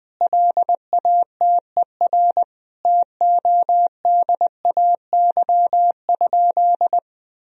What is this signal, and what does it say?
Morse code, heard as a keyed tone, about 20 wpm, 700 Hz. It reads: LATER TODAY?
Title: Sentences from Top 1000 Words - Review Part 2 - 20wpm